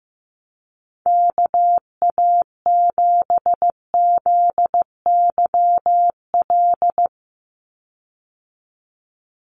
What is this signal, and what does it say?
Morse code, heard as a keyed tone, about 15 wpm, 700 Hz. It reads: KA7ZYL
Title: USA Call Signs 15wpm